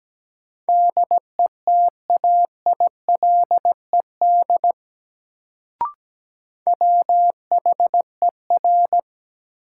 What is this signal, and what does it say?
Morse code, heard as a keyed tone, about 17 wpm, 700 Hz. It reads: DETAILED WHER
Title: Sentences from Top 1500 Words - New Word Drill - 17wpm